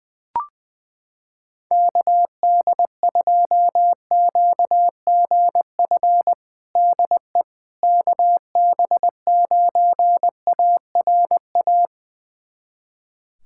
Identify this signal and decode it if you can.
Morse code, heard as a keyed tone, about 20 wpm, 700 Hz. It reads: KD2QGF DE KB9ARA
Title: USA Callsign DE Callsign - 6 Characters - 20wpm